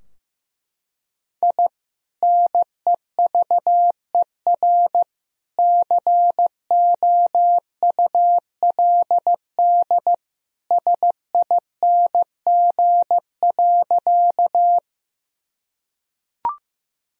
Morse code, heard as a keyed tone, about 15 wpm, 700 Hz. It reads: I NEVER COULD SING.
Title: Sentences from Top 1000 Words - Review Part 2 - 15wpm